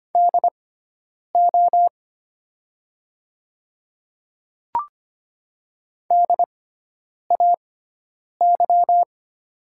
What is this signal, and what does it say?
Morse code, heard as a keyed tone, about 25 wpm, spaced as for 9 wpm, 700 Hz. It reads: DO DAY
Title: 6x Character Spacing - Top 100 Words - 25wpm